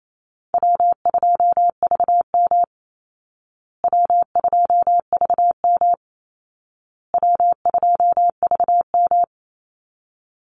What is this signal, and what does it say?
Morse code, heard as a keyed tone, about 28 wpm, 700 Hz. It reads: W2VM W2VM W2VM
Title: Call Signs Repeated 2x - 28wpm